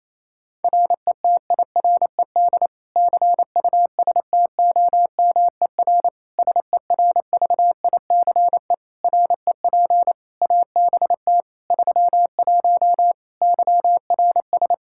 Morse code, heard as a keyed tone, about 28 wpm, 700 Hz. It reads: RETIRED CUSTOMER SERVICE REP ABT 31 YRS
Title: Practice QSO - Exchange 3 - More Info - 28wpm